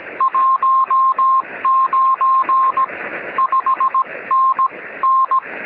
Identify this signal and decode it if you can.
Morse code, heard as a keyed tone, about 17 wpm, spaced as for 15 wpm, 1060 Hz. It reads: 19 5NN
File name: sm7lcb / 1911z dj8ms 23cm